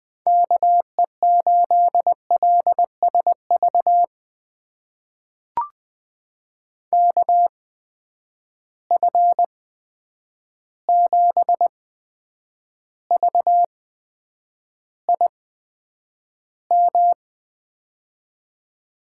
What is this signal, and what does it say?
Morse code, heard as a keyed tone, about 20 wpm, spaced as for 9 wpm, 700 Hz. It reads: KE8LSV KF7VIM
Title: USA Call Signs - 6 Characters - 8x Character Spacing - 20wpm